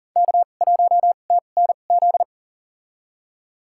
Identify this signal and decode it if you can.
Morse code, heard as a keyed tone, about 40 wpm, spaced as for 29 wpm, 700 Hz. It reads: K1TNZ
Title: USA Call Signs - 2x Character Spacing - 40wpm